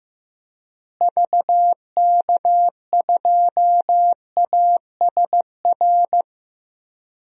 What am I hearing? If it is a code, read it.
Morse code, heard as a keyed tone, about 16 wpm, 700 Hz. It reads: VK2ASR
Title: DXCC-100 Call Signs - 6 Characters - Repeated 2x - 15wpm